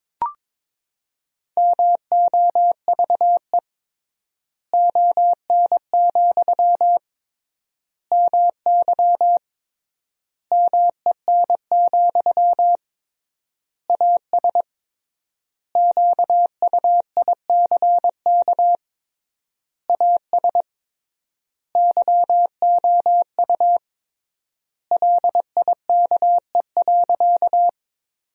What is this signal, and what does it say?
Morse code, heard as a keyed tone, about 22 wpm, 700 Hz. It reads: MOVE ON, MY MEN, AS QUICK AS YOU LIKE.
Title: 3x Word-Spacing - Sentences from Top 500 Words - 22wpm